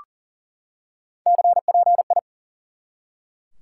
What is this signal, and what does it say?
Morse code, heard as a keyed tone, about 40 wpm, 700 Hz. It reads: CPI
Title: Top 90 QSO Elements 40wpm